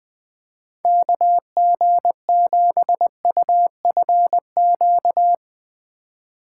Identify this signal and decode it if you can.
Morse code, heard as a keyed tone, about 20 wpm, 700 Hz. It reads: KG7UFQ
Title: Speed Racing - USA 2x3 Call Signs 20wpm